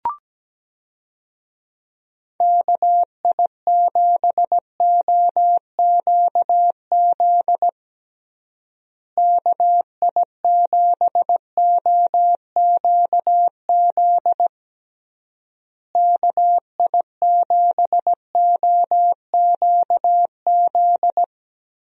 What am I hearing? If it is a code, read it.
Morse code, heard as a keyed tone, about 17 wpm, 700 Hz. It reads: KI7OQZ KI7OQZ KI7OQZ
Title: USA Call Signs - 6 Characters - Repeated 3x - 17wpm